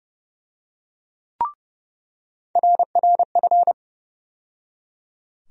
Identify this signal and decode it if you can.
Morse code, heard as a keyed tone, about 30 wpm, 700 Hz. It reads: RRF
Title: USA Call Sign Postfixes - 3 Letters 30wpm